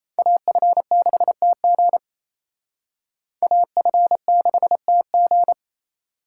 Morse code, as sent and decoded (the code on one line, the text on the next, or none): .- ..-. -.... - --. / .- ..-. -.... - --.
AF6TG AF6TG